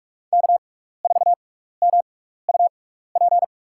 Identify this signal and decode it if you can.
Morse code, heard as a keyed tone, about 47 wpm, spaced as for 16 wpm, 700 Hz. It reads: K4MUP
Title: USA Call Signs - 5 Characters - 6x Character Spacing - 45wpm